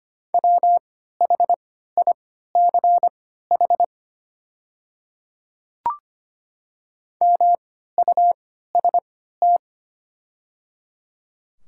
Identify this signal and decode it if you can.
Morse code, heard as a keyed tone, about 25 wpm, spaced as for 14 wpm, 690 Hz. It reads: WHICH MUST
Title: 3x Character Spacing - Top 300 Words - 25wpm